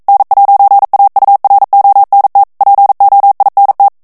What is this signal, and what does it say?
Morse code, heard as a keyed tone, about 42 wpm, 800 Hz. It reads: N'AURONT POINT
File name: CW / FICHIERS-MP3 / FRANCAIS TEXTE CLAIR 2 VITESSE 40 MOTS MINUTES